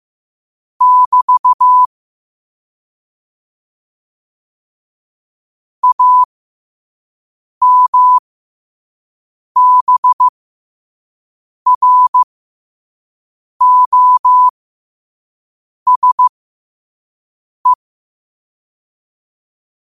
Morse code, heard as a keyed tone, about 15 wpm, spaced as for 5 wpm, 1000 Hz. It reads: <BT> AMBROSE
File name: qotdfiles / 5wpm